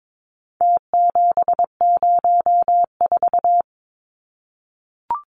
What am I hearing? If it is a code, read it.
Morse code, heard as a keyed tone, about 22 wpm, 700 Hz. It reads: T704
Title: Numbers - Contest Style - 4-Digit 22wpm